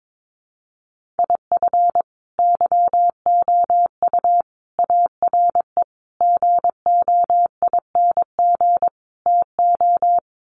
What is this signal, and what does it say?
Morse code, heard as a keyed tone, about 22 wpm, 700 Hz. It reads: IF YOU ARE GOING TO
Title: Sets of 5 Words Encore2 22wpm